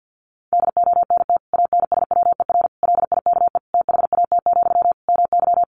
Morse code, heard as a keyed tone, about 50 wpm, 700 Hz. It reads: DONT ANSWER LIKE THAT, MY
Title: Sentences from Top 400 Words 50wpm